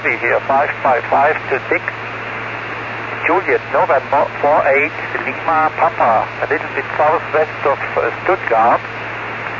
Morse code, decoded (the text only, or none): none